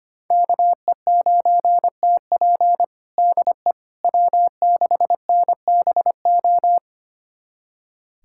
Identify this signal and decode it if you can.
Morse code, heard as a keyed tone, about 25 wpm, 700 Hz. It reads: KE9TP DE W6NBO
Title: USA Callsign DE Callsign - 5 Characters - 25wpm